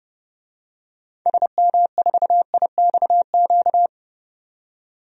Morse code, heard as a keyed tone, about 30 wpm, 700 Hz. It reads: SM4IXQ
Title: DXCC-100 Call Signs - 6 Characters - 30wpm